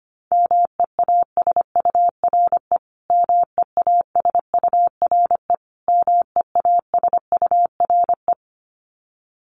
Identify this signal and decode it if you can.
Morse code, heard as a keyed tone, about 25 wpm, 700 Hz. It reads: MEASURE MEASURE MEASURE
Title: Top 500 Words Repeated 25wpm